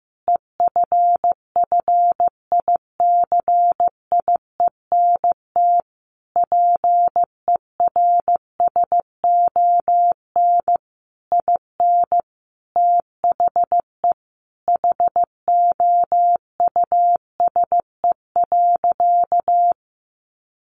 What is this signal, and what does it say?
Morse code, heard as a keyed tone, about 15 wpm, 700 Hz. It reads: EFFICIENT PERSON IN THE HOUSE.
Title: Sentences from Top 1300 Words 15wpm